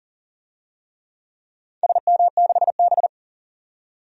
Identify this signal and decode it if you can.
Morse code, heard as a keyed tone, about 40 wpm, 700 Hz. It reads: SM6B
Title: DXCC-100 Call Signs - 4 Characters - 40wpm